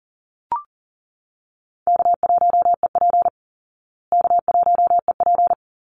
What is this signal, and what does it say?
Morse code, heard as a keyed tone, about 40 wpm, 700 Hz. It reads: K1EP K1EP
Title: Call Signs Repeated 3x - 40wpm